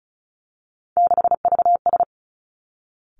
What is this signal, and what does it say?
Morse code, heard as a keyed tone, about 36 wpm, 700 Hz. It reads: 6VS
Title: Three Challenging Letter-Number Combinations 35wpm